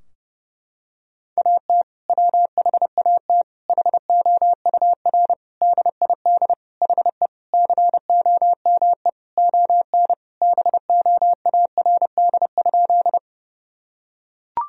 Morse code, heard as a keyed tone, about 30 wpm, 700 Hz. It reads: AT WHAT HOUR DID HE COME ON BOARD?
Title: Sentences from Top 1000 Words - Review Part 4 - 30wpm